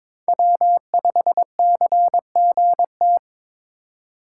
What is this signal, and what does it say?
Morse code, heard as a keyed tone, about 21 wpm, 700 Hz. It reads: W5CGT
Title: USA Call Signs - 5 Characters - 4x Character Spacing - 22wpm